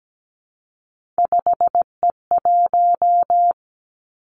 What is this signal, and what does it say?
Morse code, heard as a keyed tone, about 17 wpm, 700 Hz. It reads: 5E1
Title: Three Challenging Letter-Number Combinations 17wpm